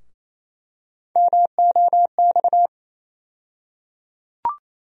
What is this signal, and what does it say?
Morse code, heard as a keyed tone, about 28 wpm, 700 Hz. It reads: MOX
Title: USA Call Sign Postfixes - 3 Letters 28wpm